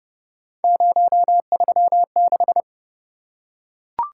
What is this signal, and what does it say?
Morse code, heard as a keyed tone, about 28 wpm, 700 Hz. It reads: 036